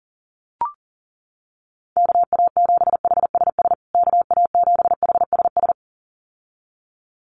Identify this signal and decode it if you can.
Morse code, heard as a keyed tone, about 40 wpm, 700 Hz. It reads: KA7HSS KA7HSS
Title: USA Call Signs Repeated 2x 40wpm